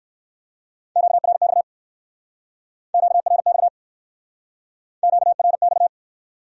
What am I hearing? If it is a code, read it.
Morse code, heard as a keyed tone, about 60 wpm, 700 Hz. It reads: QRX QRX QRX